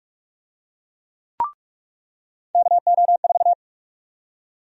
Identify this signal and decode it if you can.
Morse code, heard as a keyed tone, about 46 wpm, 700 Hz. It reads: KO4
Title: USA Call Sign Prefixes - 45wpm